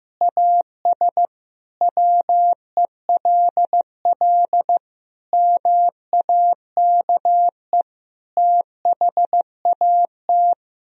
Morse code, heard as a keyed tone, about 15 wpm, 700 Hz. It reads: AS WELL MAKE THAT
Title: Sentences from Top 1000 Words - Review Part 1 - 15wpm